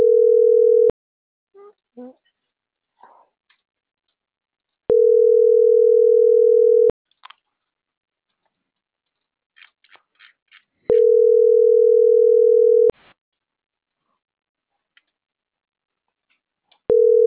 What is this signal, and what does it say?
Ringback tone (North America), 440 + 480 Hz.
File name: CENTRAL MAERSK